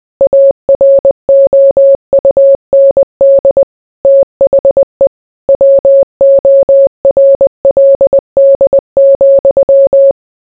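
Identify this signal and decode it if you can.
Morse code, heard as a keyed tone, about 20 wpm, 550 Hz. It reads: AROUND THE WORLD,
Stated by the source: Morse code mixed with noise and music as background sound for sleep.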